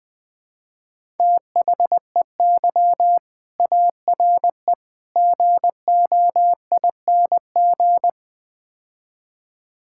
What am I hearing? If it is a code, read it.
Morse code, heard as a keyed tone, about 21 wpm, 700 Hz. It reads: THEY ARE GOING
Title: Build A Phrase - 4-Word Phrases - 20wpm